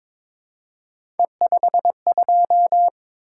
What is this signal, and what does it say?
Morse code, heard as a keyed tone, about 22 wpm, 700 Hz. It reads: E52
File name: practice / 22wpm / Three Letter-Number Combinations - 22wpm